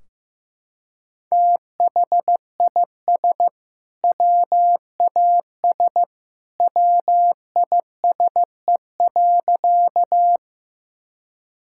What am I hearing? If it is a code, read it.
Morse code, heard as a keyed tone, about 15 wpm, 700 Hz. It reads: THIS WAS WISE.
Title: Sentences from Top 1600 Words 15wpm